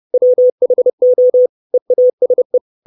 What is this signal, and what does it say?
Morse code, heard as a keyed tone, about 30 wpm, 500 Hz. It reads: WHO EASE
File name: 632 Words 30WPM